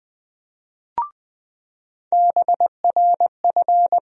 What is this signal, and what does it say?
Morse code, heard as a keyed tone, about 20 wpm, 700 Hz. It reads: BRF
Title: USA Call Sign Postfixes - 3 Letters 20wpm